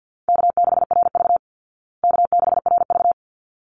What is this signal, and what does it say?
Morse code, heard as a keyed tone, about 49 wpm, 700 Hz. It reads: K6RV K6RV
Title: Easy Call Signs Repeated 2x - 50wpm